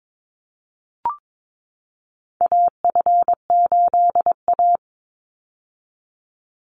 Morse code, heard as a keyed tone, about 22 wpm, 700 Hz. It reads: AF8A
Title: Call Signs - 22wpm